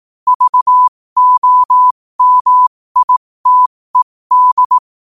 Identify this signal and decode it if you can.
Morse code, heard as a keyed tone, about 18 wpm, spaced as for 15 wpm, 1000 Hz. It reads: VOMITED
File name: qotdfiles / 15wpm